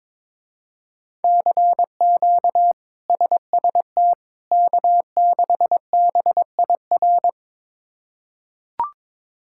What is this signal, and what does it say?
Morse code, heard as a keyed tone, about 22 wpm, 700 Hz. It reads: CQ SST K6BIR
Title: K1USN SST QSO Practice - 22wpm